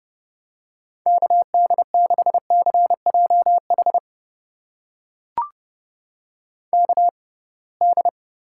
Morse code, heard as a keyed tone, about 30 wpm, spaced as for 23 wpm, 700 Hz. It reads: KD6CJH KD
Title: USA Call Signs - 6 Characters - 6x Character Spacing - 30wpm